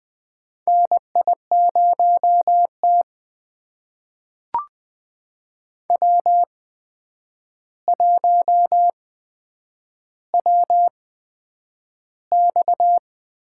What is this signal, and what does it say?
Morse code, heard as a keyed tone, about 20 wpm, spaced as for 9 wpm, 700 Hz. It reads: NI0T W1WX
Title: USA Call Signs - 4 Characters - 8x Character Spacing - 20wpm